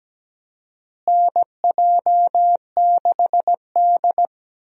Morse code, heard as a keyed tone, about 17 wpm, 700 Hz. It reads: NJ6D
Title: USA Call Signs - 4 Characters - Repeated 2x - 17wpm